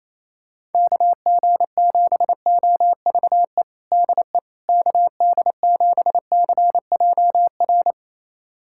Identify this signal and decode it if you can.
Morse code, heard as a keyed tone, about 28 wpm, 700 Hz. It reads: KG7OVE DE KD7CJR
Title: USA Callsign DE Callsign 28wpm